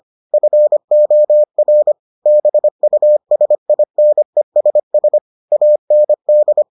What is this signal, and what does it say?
Morse code, heard as a keyed tone, about 25 wpm, 600 Hz. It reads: FOR BUSINESS AND